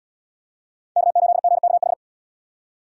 Morse code, heard as a keyed tone, about 72 wpm, 700 Hz. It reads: K8GKH